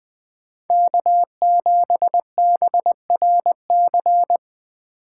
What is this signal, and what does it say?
Morse code, heard as a keyed tone, about 20 wpm, 700 Hz. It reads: K7BRC
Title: USA Call Signs - 5 Characters - 4x Character Spacing - 20wpm